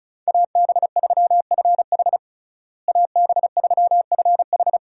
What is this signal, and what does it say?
Morse code, heard as a keyed tone, about 36 wpm, 700 Hz. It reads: AB3FH AB3FH
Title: USA Call Signs - 5 Characters - Repeated 2x - 35wpm